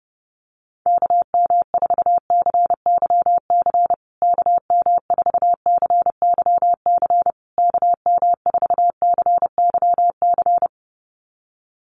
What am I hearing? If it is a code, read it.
Morse code, heard as a keyed tone, about 30 wpm, 700 Hz. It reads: KM4CYC KM4CYC KM4CYC
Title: USA Call Signs Repeated 3x 30wpm